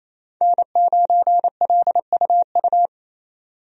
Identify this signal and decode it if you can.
Morse code, heard as a keyed tone, about 28 wpm, 700 Hz. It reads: N9LUU
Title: USA Call Signs - 5 Characters - 28wpm